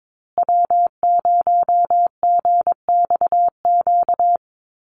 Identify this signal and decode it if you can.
Morse code, heard as a keyed tone, about 22 wpm, 700 Hz. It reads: W0GXQ
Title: Call Signs Repeated 3x - 22wpm